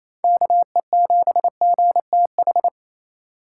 Morse code, heard as a keyed tone, about 28 wpm, 700 Hz. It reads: KE7GTH
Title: USA Call Signs - 6 Characters - 3x Character Spacing - 28wpm